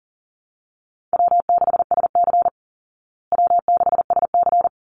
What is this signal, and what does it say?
Morse code, heard as a keyed tone, about 40 wpm, 700 Hz. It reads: W6SC W6SC
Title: Call Signs Repeated 3x - 40wpm